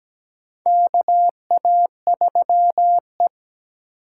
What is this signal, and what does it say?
Morse code, heard as a keyed tone, about 17 wpm, 700 Hz. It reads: KA3E
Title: USA Call Signs - 6x Character Spacing - 17wpm